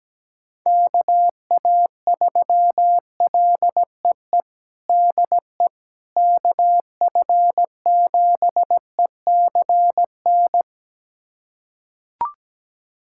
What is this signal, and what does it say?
Morse code, heard as a keyed tone, about 17 wpm, 700 Hz. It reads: KA3LEE DE KF7ECN
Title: USA Callsign DE Callsign - 6 Characters - 17wpm